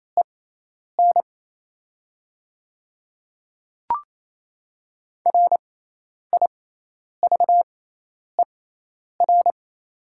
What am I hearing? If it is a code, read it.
Morse code, heard as a keyed tone, about 28 wpm, spaced as for 10 wpm, 700 Hz. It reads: EN RIVER
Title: 6x Character Spacing - Top 300 Words - 28wpm